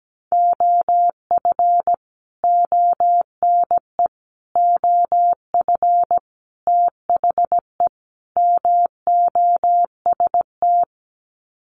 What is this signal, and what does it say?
Morse code, heard as a keyed tone, about 17 wpm, 700 Hz. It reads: OF ONE OF THE MOST
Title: Sets of 5 Words Encore2 17wpm